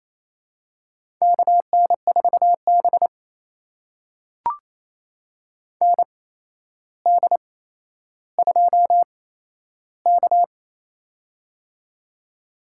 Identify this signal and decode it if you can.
Morse code, heard as a keyed tone, about 29 wpm, spaced as for 12 wpm, 700 Hz. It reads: KN4B ND2K
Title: USA Call Signs - 4 Characters - 8x Character Spacing - 28wpm